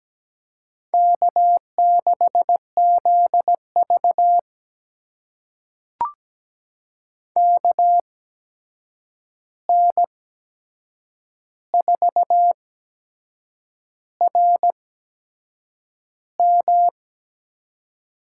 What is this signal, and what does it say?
Morse code, heard as a keyed tone, about 17 wpm, spaced as for 7 wpm, 700 Hz. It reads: K6ZV KN4RM
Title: USA Call Signs - 8x Character Spacing - 17wpm